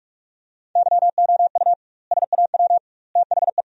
Morse code, heard as a keyed tone, about 46 wpm, 700 Hz. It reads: YOU SAW THE
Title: 2x Word-Spacing - Sentences from Top 400 Words - 45wpm